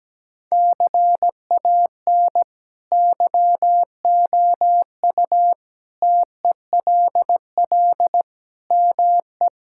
Morse code, heard as a keyed tone, about 17 wpm, 700 Hz. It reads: CAN YOU TELL ME
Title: Sentences from Top 1600 Words 17wpm